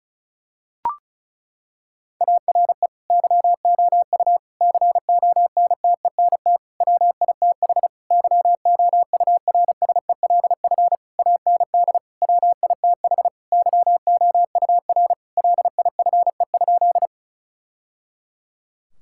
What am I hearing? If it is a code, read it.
Morse code, heard as a keyed tone, about 36 wpm, 700 Hz. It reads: ARE YOU CONTENT WITH YOURSELF AND WITH YOUR LIFE?
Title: Sentences from Top 1000 Words - Review Part 6 - 35wpm